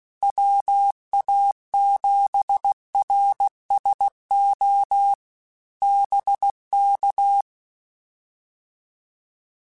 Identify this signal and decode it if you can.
Morse code, heard as a keyed tone, about 16 wpm, 800 Hz. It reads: WA7RSO BK
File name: QSOs / QSOs-mp3s / WD0BGP 1@15